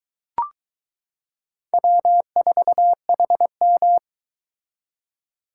Morse code, heard as a keyed tone, about 22 wpm, 700 Hz. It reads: W4HM